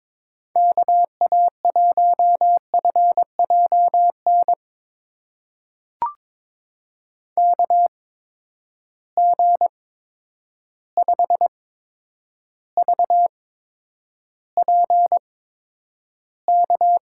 Morse code, heard as a keyed tone, about 22 wpm, spaced as for 9 wpm, 700 Hz. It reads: KA1FJN KG5VPK
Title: USA Call Signs - 6 Characters - 8x Character Spacing - 22wpm